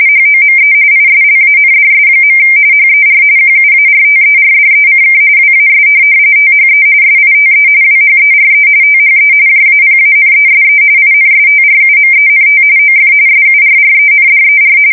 RTTY, about 45.45 baud. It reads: HOME ISLANDS. HER QUARTERDECK WAS THE SITE WHERE THE JAPANESE INSTRUMENT OF SURRENDER
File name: fsk170 / USS-Missouri